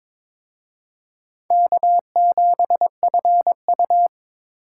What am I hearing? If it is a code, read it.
Morse code, heard as a keyed tone, about 26 wpm, 700 Hz. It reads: K7FU
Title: Speed Racing - USA 1x2 Call Signs 22wpm